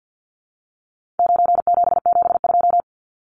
Morse code, heard as a keyed tone, about 50 wpm, 700 Hz. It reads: 9772
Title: Numbers - 4-Digit 50wpm